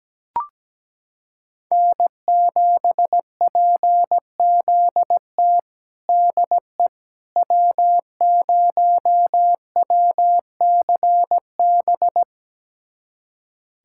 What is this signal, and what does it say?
Morse code, heard as a keyed tone, about 17 wpm, 700 Hz. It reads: N7PZT DE W0WCB
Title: USA Callsign DE Callsign 17wpm